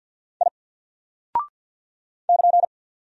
Morse code, heard as a keyed tone, about 48 wpm, 700 Hz. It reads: I <KN>
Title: Single Character (Letter:Number:Punctuation:Prosign) Alt 50wpm